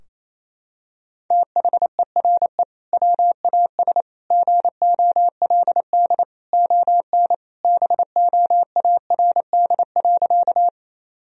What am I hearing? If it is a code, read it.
Morse code, heard as a keyed tone, about 28 wpm, 700 Hz. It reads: THERE WAS GOLD ON BOARD.
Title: Sentences from Top 1000 Words - Review Part 4 - 28wpm